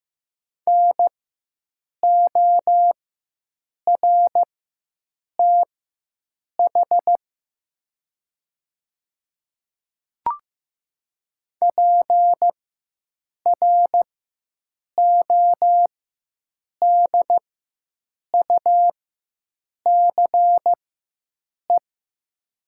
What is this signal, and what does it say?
Morse code, heard as a keyed tone, about 15 wpm, spaced as for 7 wpm, 700 Hz. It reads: NORTH PRODUCE
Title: 4x Character Spacing - Top 500 Words - 15wpm